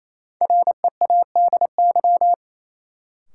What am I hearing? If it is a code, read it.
Morse code, heard as a keyed tone, about 28 wpm, 700 Hz. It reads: READY
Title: Sentences from Top 400 Words - New Word Drill - 28wpm